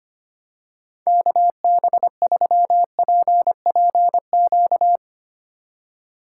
Morse code, heard as a keyed tone, about 25 wpm, 700 Hz. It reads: KB3PPQ
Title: USA Call Signs - 2x Character Spacing - 25wpm